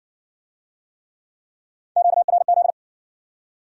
Morse code, heard as a keyed tone, about 59 wpm, 700 Hz. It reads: QRZ